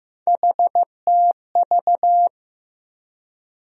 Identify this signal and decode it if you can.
Morse code, heard as a keyed tone, about 15 wpm, 700 Hz. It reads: HTV
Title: USA Call Sign Postfixes - 3 Letters 15wpm